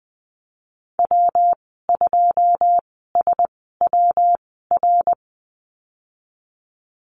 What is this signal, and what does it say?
Morse code, heard as a keyed tone, about 20 wpm, spaced as for 15 wpm, 700 Hz. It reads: W2SWR
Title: 161 Lesson 26g - Letter g - Call Signs 20wpm